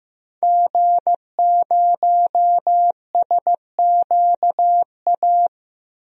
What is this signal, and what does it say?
Morse code, heard as a keyed tone, about 16 wpm, 700 Hz. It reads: G0SQA
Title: DXCC-100 Call Signs - 5 Characters - Repeated 2x - 15wpm